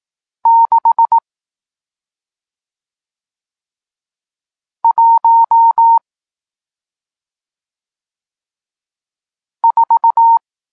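Morse code, heard as a keyed tone, about 18 wpm, spaced as for 2 wpm, 920 Hz. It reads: 614
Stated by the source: Morse code lesson teaching digits 1, 4 and 6, from SP9MOA, Niepołomicki Klub Krótkofalowców (Poland) amateur radio club.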